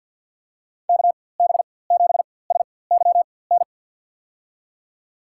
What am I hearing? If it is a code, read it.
Morse code, heard as a keyed tone, about 48 wpm, spaced as for 23 wpm, 700 Hz. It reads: KB7SYN
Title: USA Call Signs - 6 Characters - 4x Character Spacing - 50wpm